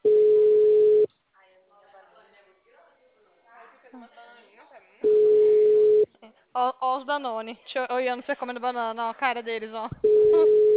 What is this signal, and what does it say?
Ringback tone (Europe, 425 Hz), with speech.
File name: out- -2006- - - .46897